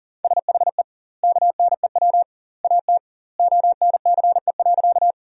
Morse code, heard as a keyed tone, about 40 wpm, 700 Hz. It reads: SHE KNEW AT ONCE.
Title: 2x Word-Spacing - Sentences from Top 400 Words - 40wpm